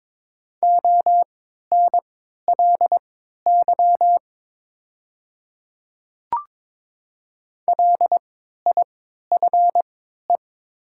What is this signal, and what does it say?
Morse code, heard as a keyed tone, about 22 wpm, spaced as for 12 wpm, 700 Hz. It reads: ONLY LIFE
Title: 3x Character Spacing - Top 300 Words - 22wpm